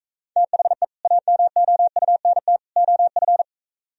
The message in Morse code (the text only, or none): THE AMOUNT OF